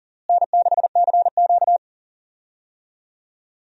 Morse code, heard as a keyed tone, about 37 wpm, 700 Hz. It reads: N6CQ
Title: USA Call Signs - 4 Characters - Repeated 2x - 40wpm